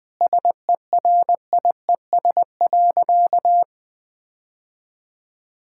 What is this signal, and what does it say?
Morse code, heard as a keyed tone, about 21 wpm, 700 Hz. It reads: SERIES.
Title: Sentences from Top 1000 Words - Review Part 5 - 20wpm